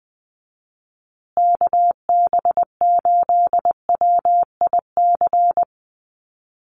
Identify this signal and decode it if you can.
Morse code, heard as a keyed tone, about 20 wpm, 700 Hz. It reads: KB8WIC
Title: USA Call Signs Repeated 2x 20wpm